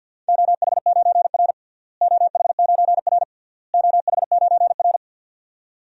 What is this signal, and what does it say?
Morse code, heard as a keyed tone, about 52 wpm, 700 Hz. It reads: OH9R OH9R OH9R
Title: DXCC-100 Call Signs - 4 Characters - Repeated 3x - 50wpm